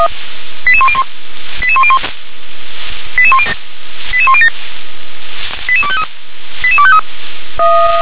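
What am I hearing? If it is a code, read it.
Five-tone selective call (ZVEI), heard as quick runs of five tones, tones of about 70 ms. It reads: 88101 88101 88107 88253 88253
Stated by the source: ZVEI five-tone selective call received on SDR.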